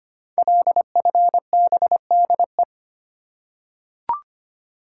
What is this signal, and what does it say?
Morse code, heard as a keyed tone, about 25 wpm, 700 Hz. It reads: LFBDE
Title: Code Groups - 5 Letters - 25wpm